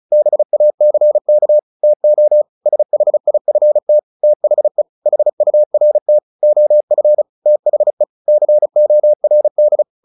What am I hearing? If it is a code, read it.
Morse code, heard as a keyed tone, about 34 wpm, 600 Hz. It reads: BACK TO SHIFT THE HURT OF THE CORD
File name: the old man and the sea / the old man and the sea 1 35wpm